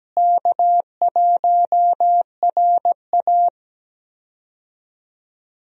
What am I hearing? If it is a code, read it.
Morse code, heard as a keyed tone, about 17 wpm, 700 Hz. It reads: K1RA
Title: USA Call Signs - 4 Characters - Repeated 2x - 17wpm